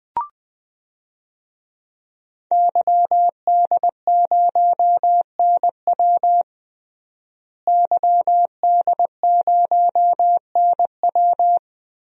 Morse code, heard as a keyed tone, about 20 wpm, 700 Hz. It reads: YD0NW YD0NW
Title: DXCC-100 Call Signs - 5 Characters - Repeated 3x - 20wpm